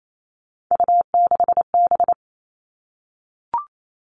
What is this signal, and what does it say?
Morse code, heard as a keyed tone, about 28 wpm, 700 Hz. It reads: U6B
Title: All Cut Numbers - 3-Digit 28wpm